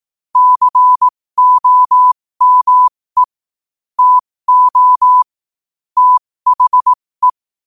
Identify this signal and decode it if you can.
Morse code, heard as a keyed tone, about 18 wpm, spaced as for 15 wpm, 1000 Hz. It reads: COME TO THE
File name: qotdfiles / 15wpm